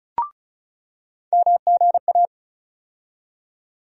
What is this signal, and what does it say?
Morse code, heard as a keyed tone, about 36 wpm, 700 Hz. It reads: MGA
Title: USA Call Sign Postfixes - 3 Letters 35wpm